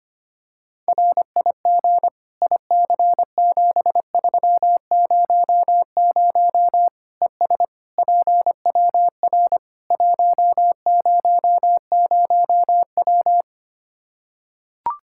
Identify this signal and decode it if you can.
Morse code, heard as a keyed tone, about 25 wpm, 700 Hz. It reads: RIG IC7300 ES PWR 100W
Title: Practice QSO - Exchange 2 - Info - 25wpm